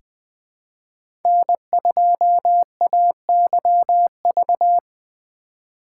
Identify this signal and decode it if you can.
Morse code, heard as a keyed tone, about 20 wpm, 700 Hz. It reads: N2AYV
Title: USA Call Signs - 5 Characters - 3x Character Spacing - 20wpm